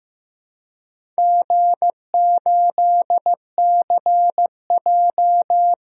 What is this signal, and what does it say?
Morse code, heard as a keyed tone, about 15 wpm, 700 Hz. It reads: G8CJ
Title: DXCC-100 Call Signs - 4 Characters - 15wpm